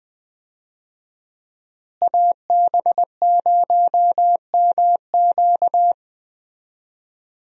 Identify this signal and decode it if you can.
Morse code, heard as a keyed tone, about 20 wpm, 700 Hz. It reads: AB0MQ
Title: USA Call Signs - 5 Characters - Repeated 2x - 20wpm